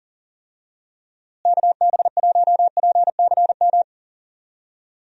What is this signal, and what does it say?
Morse code, heard as a keyed tone, about 40 wpm, 700 Hz. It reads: KB1PCM
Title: USA Call Signs - 6 Characters - Repeated 3x - 40wpm